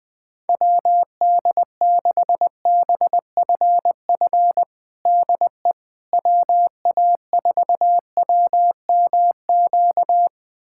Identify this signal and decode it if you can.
Morse code, heard as a keyed tone, about 21 wpm, 700 Hz. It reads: WD6BFF DE WA4WMQ
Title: USA Callsign DE Callsign - 6 Characters - 20wpm